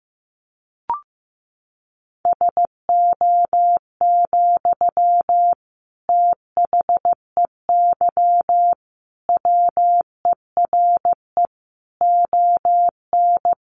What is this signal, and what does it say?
Morse code, heard as a keyed tone, about 15 wpm, 700 Hz. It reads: SO, THEY WERE ON
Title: Sentences from Top 1400 Words 15wpm